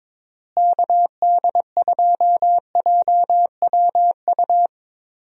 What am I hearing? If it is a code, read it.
Morse code, heard as a keyed tone, about 21 wpm, 700 Hz. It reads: KD2JWU
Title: USA Call Signs - 6 Characters - Repeated 3x - 22wpm